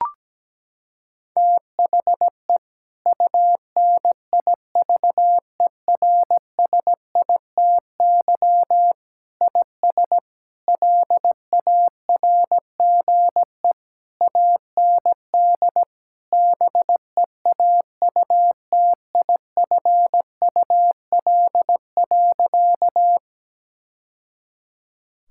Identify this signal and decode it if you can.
Morse code, heard as a keyed tone, about 17 wpm, 700 Hz. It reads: THE UNIVERSITY IS LARGE AND BEAUTIFUL.
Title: Sentences from Top 1000 Words - Review Part 5 - 17wpm